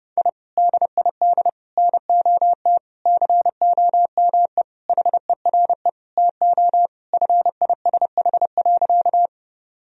Morse code, heard as a keyed tone, about 32 wpm, 700 Hz. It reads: I DID NOT COME HERE TO FISH.
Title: Sentences from Top 1000 Words - Review Part 1 - 30wpm